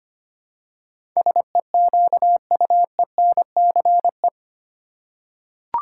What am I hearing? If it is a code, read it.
Morse code, heard as a keyed tone, about 25 wpm, 700 Hz. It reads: SEQUENCE
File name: Eight Letter Words - 25wpm